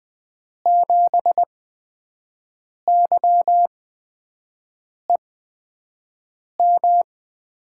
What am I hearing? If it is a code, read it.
Morse code, heard as a keyed tone, about 20 wpm, spaced as for 5 wpm, 700 Hz. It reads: 7YEM
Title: USA Call Signs - 6 Characters - 8x Character Spacing - 20wpm